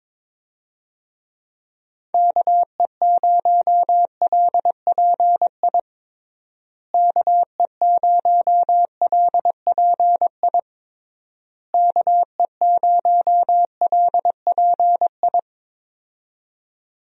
Morse code, heard as a keyed tone, about 22 wpm, 700 Hz. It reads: KE0LPI KE0LPI KE0LPI
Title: USA Call Signs - 6 Characters - Repeated 3x - 22wpm